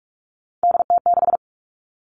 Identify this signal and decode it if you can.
Morse code, heard as a keyed tone, about 44 wpm, 700 Hz. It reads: DT6
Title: All Cut Numbers - 3-Digit 45wpm